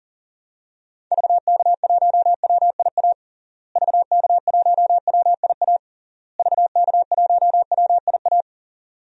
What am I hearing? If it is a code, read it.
Morse code, heard as a keyed tone, about 40 wpm, 700 Hz. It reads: VK1WIA VK1WIA VK1WIA
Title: DXCC-100 Call Signs - 6 Characters - Repeated 3x - 40wpm